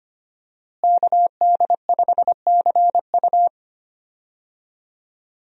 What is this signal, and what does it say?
Morse code, heard as a keyed tone, about 25 wpm, 700 Hz. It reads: KD5CU
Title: USA Call Signs - 5 Characters - Repeated 3x - 25wpm